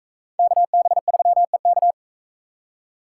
Morse code, heard as a keyed tone, about 36 wpm, 700 Hz. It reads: KB3EK